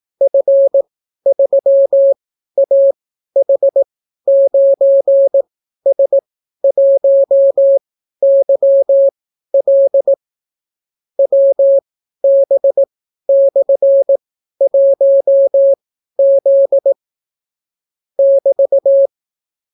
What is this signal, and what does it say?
Morse code, heard as a keyed tone, about 18 wpm, spaced as for 12 wpm, 550 Hz. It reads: F3AH9S1YL WB/1Z <BT>